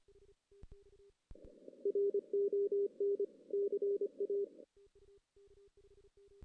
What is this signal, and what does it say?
Morse code, heard as a keyed tone, about 25 wpm, 410 Hz. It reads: RON CA
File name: 2016 JAN CW NAQP